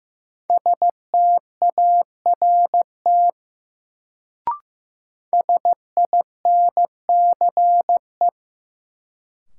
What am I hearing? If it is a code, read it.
Morse code, heard as a keyed tone, about 15 wpm, 700 Hz. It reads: START SINCE
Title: Sentences from Top 300 Words - New Word Drill - 15wpm